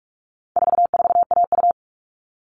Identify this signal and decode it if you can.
Morse code, heard as a keyed tone, about 44 wpm, 700 Hz. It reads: 44AU
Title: All Cut Numbers - 4-Digit 45wpm